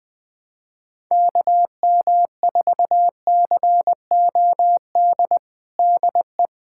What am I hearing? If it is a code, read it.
Morse code, heard as a keyed tone, about 20 wpm, 700 Hz. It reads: KM4COD DE
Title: USA Callsign DE Callsign - 6 Characters - 20wpm